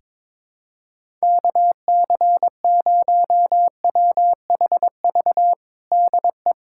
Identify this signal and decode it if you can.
Morse code, heard as a keyed tone, about 22 wpm, 700 Hz. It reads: KC0WHV DE
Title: USA Callsign DE Callsign 22wpm